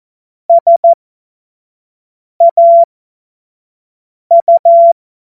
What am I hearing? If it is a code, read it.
Morse code, heard as a keyed tone, about 14 wpm, spaced as for 5 wpm, 680 Hz. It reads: SAU
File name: Morsekurs / Witze / Witz 21. September 2021, 25 F70